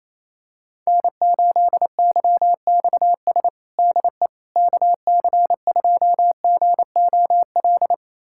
Morse code, heard as a keyed tone, about 28 wpm, 700 Hz. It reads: N8YXS DE KC2GOL
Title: USA Callsign DE Callsign 28wpm